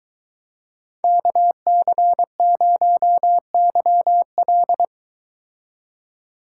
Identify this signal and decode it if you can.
Morse code, heard as a keyed tone, about 22 wpm, 700 Hz. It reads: KC0YL